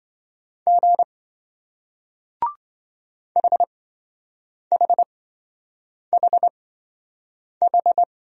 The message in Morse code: G HHHH

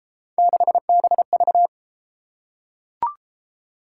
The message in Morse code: 6BV

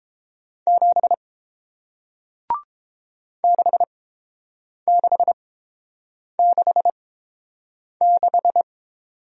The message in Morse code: 7 6666

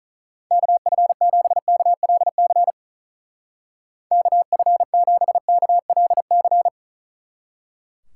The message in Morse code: KF7KLC KF7KLC

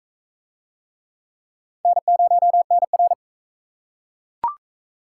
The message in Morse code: N0NR

